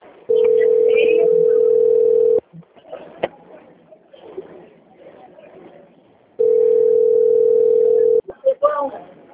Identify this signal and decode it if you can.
Ringback tone (North America), 440 + 480 Hz, with speech.
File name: out- - - - - .20158